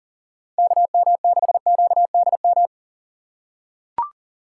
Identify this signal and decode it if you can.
Morse code, heard as a keyed tone, about 39 wpm, 700 Hz. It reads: KM6QDM
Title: USA Call Signs - 6 Characters - 2x Character Spacing - 40wpm